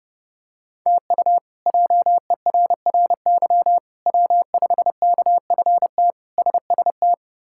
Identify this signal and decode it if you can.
Morse code, heard as a keyed tone, about 29 wpm, 700 Hz. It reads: TU JERRY W5KFT SST
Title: K1USN SST QSO Practice - 30wpm